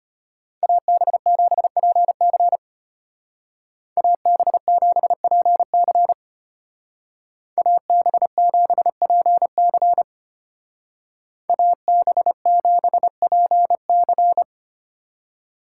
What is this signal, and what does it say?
Morse code, heard as a keyed tone, about 32 wpm, 700 Hz. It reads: AB7PC AB7PC AB7PC AB7PC